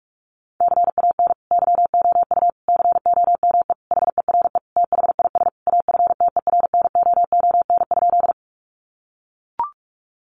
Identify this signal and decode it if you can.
Morse code, heard as a keyed tone, about 44 wpm, 700 Hz. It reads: CAN YOU COME HERE THIS AFTERNOON?
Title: Sentences from Top 1500 Words 45wpm